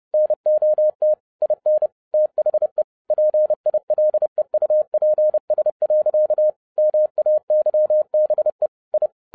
Morse code, heard as a keyed tone, about 30 wpm, 610 Hz. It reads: NOT IN THE PILEUPS. MAYBE I